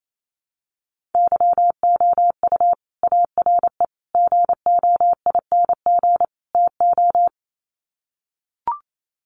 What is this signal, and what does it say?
Morse code, heard as a keyed tone, about 28 wpm, 700 Hz. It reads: YOU ARE GOING TO
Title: Sets of 4 Words 28wpm